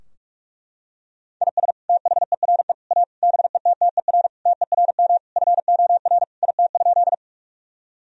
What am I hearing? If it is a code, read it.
Morse code, heard as a keyed tone, about 41 wpm, 700 Hz. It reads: IS THERE A BETTER TERM FOR IT?
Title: Sentences from Top 1000 Words - Review Part 4 - 45wpm